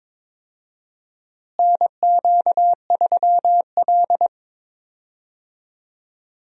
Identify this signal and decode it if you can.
Morse code, heard as a keyed tone, about 22 wpm, 700 Hz. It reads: NQ3L
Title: USA Call Signs - 4 Characters - Repeated 2x - 22wpm